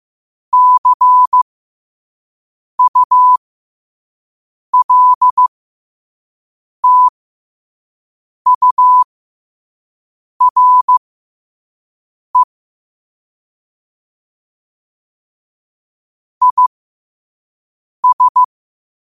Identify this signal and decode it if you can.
Morse code, heard as a keyed tone, about 15 wpm, spaced as for 5 wpm, 1000 Hz. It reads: CULTURE IS
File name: qotdfiles / 5wpm